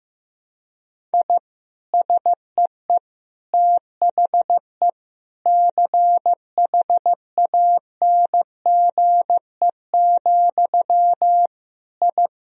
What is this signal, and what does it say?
Morse code, heard as a keyed tone, about 15 wpm, 700 Hz. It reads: I SEE THE CHANGE, I
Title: Sentences from Top 1000 Words - Review Part 1 - 15wpm